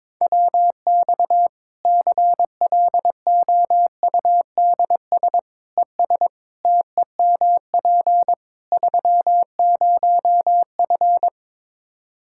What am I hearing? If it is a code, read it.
Morse code, heard as a keyed tone, about 22 wpm, 700 Hz. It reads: WX CLOUDS ES TEMP 30F
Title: K1USN SST QSO Practice 22wpm